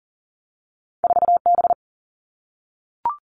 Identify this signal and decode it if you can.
Morse code, heard as a keyed tone, about 43 wpm, 700 Hz. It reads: 4 B
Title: Two Challenging Letter-Number Combinations 40wpm